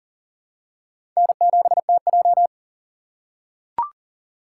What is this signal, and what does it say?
Morse code, heard as a keyed tone, about 39 wpm, 700 Hz. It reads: N7TJ
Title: USA Call Signs - 4 Characters - 3x Character Spacing - 40wpm